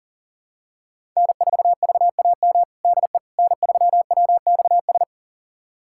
Morse code, heard as a keyed tone, about 40 wpm, 700 Hz. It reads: N4VAM DE N3WXS
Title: USA Callsign DE Callsign - 5 Characters - 40wpm